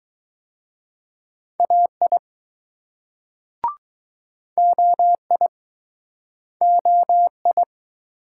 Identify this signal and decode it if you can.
Morse code, heard as a keyed tone, about 20 wpm, 700 Hz. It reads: AI OI OI